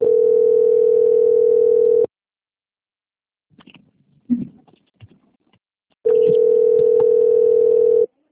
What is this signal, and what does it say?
Ringback tone (North America), 440 + 480 Hz.